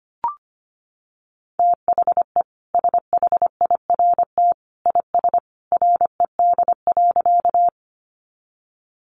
Morse code, heard as a keyed tone, about 25 wpm, 700 Hz. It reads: THE SHIRT IS RED.
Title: Additional Easy Phrases 25wpm